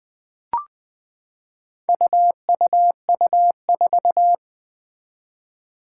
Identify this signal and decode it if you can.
Morse code, heard as a keyed tone, about 20 wpm, 700 Hz. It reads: UUU4
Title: Four Challenging Letter-Number Combinations 20wpm